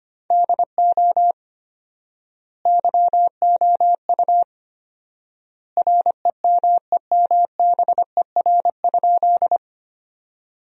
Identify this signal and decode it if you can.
Morse code, heard as a keyed tone, about 25 wpm, 700 Hz. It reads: DO YOU REMEMBER?
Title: 4x Word-Spacing - Sentences from Top 400 Words - 25wpm